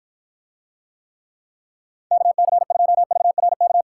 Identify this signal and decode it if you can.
Morse code, heard as a keyed tone, about 52 wpm, 700 Hz. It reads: KC2VRK